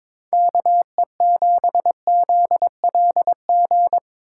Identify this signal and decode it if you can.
Morse code, heard as a keyed tone, about 25 wpm, 700 Hz. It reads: KE7ZLG
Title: Speed Racing - USA Call Signs 22wpm